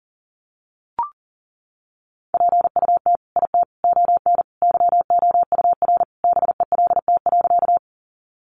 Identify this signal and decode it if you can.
Morse code, heard as a keyed tone, about 40 wpm, 700 Hz. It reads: PUT IT ON YOUR BELT.
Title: Sentences from Top 1400 Words 40wpm